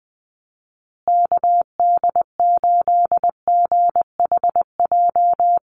Morse code, heard as a keyed tone, about 20 wpm, 700 Hz. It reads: KD8GHJ
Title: USA Call Signs Repeated 2x 20wpm